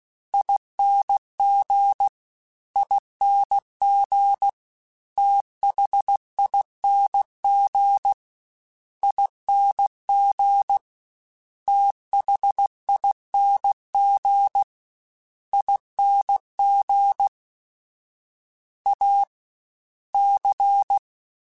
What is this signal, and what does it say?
Morse code, heard as a keyed tone, about 16 wpm, 800 Hz. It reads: ING ING THING ING THING ING A C
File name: Special INGs-mp3s / Special INGs-mp3s / SlowToFaster ing @15